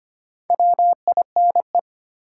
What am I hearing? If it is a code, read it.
Morse code, heard as a keyed tone, about 25 wpm, 700 Hz. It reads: WINE
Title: Four Letter Words - 25wpm